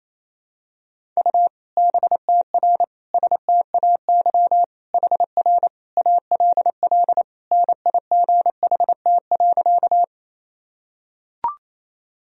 Morse code, heard as a keyed tone, about 28 wpm, 700 Hz. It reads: U BTR STAY HR ALL NIGHT.
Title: Sentences from Top 500 Words and Abbreviations 28wpm